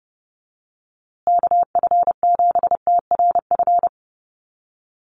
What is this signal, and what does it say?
Morse code, heard as a keyed tone, about 30 wpm, 700 Hz. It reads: KF7TRF
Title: USA Call Signs Repeated 2x 30wpm